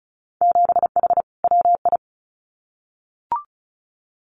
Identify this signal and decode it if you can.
Morse code, heard as a keyed tone, about 36 wpm, 700 Hz. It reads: 7H WI
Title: Winter Field Day Exchanges Encore 35wpm